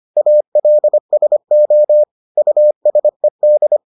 Morse code, heard as a keyed tone, about 25 wpm, 600 Hz. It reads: ALSO USED